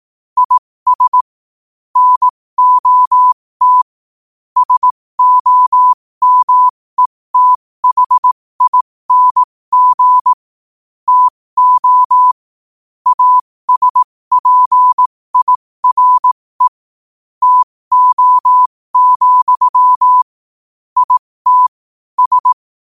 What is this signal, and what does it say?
Morse code, heard as a keyed tone, about 18 wpm, spaced as for 15 wpm, 1000 Hz. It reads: IS NOT SOMETHING TO ASPIRE TO, IT S
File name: qotdfiles / 15wpm